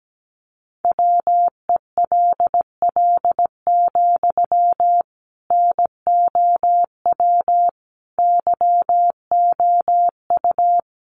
Morse code, heard as a keyed tone, about 17 wpm, 700 Hz. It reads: WELL, NOW YOU
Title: Sentences from Top 600 Words 17wpm